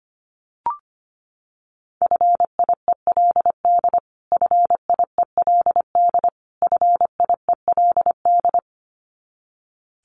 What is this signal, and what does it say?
Morse code, heard as a keyed tone, about 25 wpm, 700 Hz. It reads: FIELD FIELD FIELD
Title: Top 500 Words Repeated 25wpm